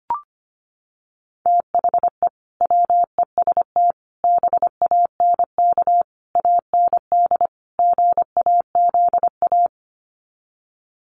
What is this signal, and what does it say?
Morse code, heard as a keyed tone, about 25 wpm, 700 Hz. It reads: THE WEST BANK AND GAZA
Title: Sets of 5 Words 25wpm